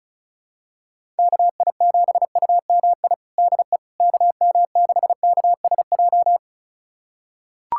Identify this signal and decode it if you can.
Morse code, heard as a keyed tone, about 34 wpm, 700 Hz. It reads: KI7UMI DE KM6KSJ
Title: USA Callsign DE Callsign - 6 Characters - 35wpm